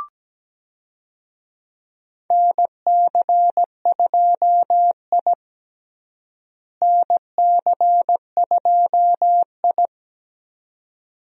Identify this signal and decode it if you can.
Morse code, heard as a keyed tone, about 17 wpm, 700 Hz. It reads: NC2I NC2I
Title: USA Call Signs - 4 Characters - Repeated 2x - 17wpm